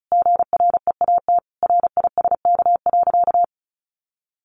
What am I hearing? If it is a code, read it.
Morse code, heard as a keyed tone, about 34 wpm, 700 Hz. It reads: GREAT RISK.
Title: Sentences from Top 600 Words 35wpm